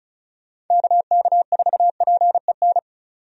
Morse code, heard as a keyed tone, about 36 wpm, 700 Hz. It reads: KK4PEN
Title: USA Call Signs - 6 Characters - 35wpm